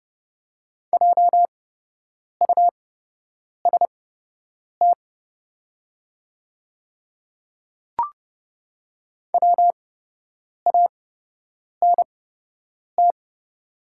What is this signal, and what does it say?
Morse code, heard as a keyed tone, about 28 wpm, spaced as for 8 wpm, 690 Hz. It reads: JUST WANT
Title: 8x Character Spacing - Top 200 Words - 30wpm